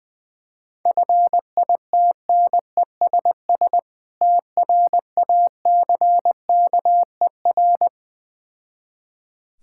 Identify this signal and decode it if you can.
Morse code, heard as a keyed tone, about 21 wpm, 700 Hz. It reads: FITNESS TRACKER
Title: Difficult Words with Hints - 20wpm